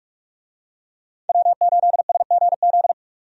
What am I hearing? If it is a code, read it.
Morse code, heard as a keyed tone, about 46 wpm, 700 Hz. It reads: W8SGZ
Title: USA Call Signs - 5 Characters - 45wpm